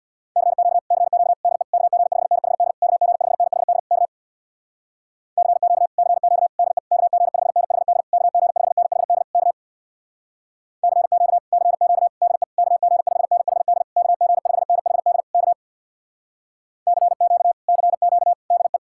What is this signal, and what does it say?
Morse code, heard as a keyed tone, about 64 wpm, 700 Hz. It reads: CQ CQ DE KG5NHD KG5NHD K CQ CQ DE KG5NHD KG5NHD K CQ CQ DE KG5NHD KG5NHD K CQ CQ DE